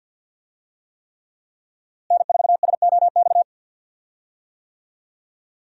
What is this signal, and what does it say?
Morse code, heard as a keyed tone, about 49 wpm, 700 Hz. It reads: N4SOX
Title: USA Call Signs - 5 Characters - Repeated 3x - 50wpm